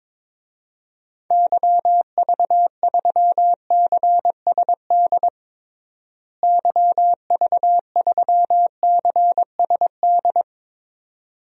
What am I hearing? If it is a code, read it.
Morse code, heard as a keyed tone, about 21 wpm, 700 Hz. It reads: YV3CSD YV3CSD
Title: DXCC-100 Call Signs - 6 Characters - Repeated 2x - 22wpm